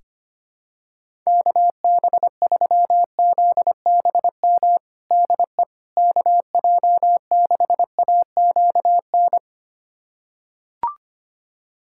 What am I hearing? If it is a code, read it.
Morse code, heard as a keyed tone, about 25 wpm, 700 Hz. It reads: KB3ZBM DE KJ6AQN
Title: USA Callsign DE Callsign - 6 Characters - 25wpm